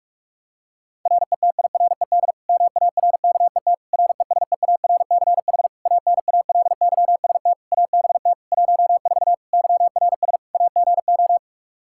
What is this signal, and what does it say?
Morse code, heard as a keyed tone, about 46 wpm, 700 Hz. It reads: RETIRED MARKET RESEARCH ANALYST ABT 14 YRS AGO
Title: Practice QSO - Exchange 3 - More Info - 45wpm